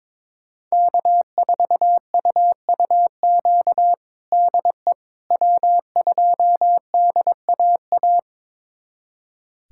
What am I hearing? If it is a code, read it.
Morse code, heard as a keyed tone, about 22 wpm, 700 Hz. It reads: K4UUQ DE W2DAA
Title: USA Callsign DE Callsign - 5 Characters - 22wpm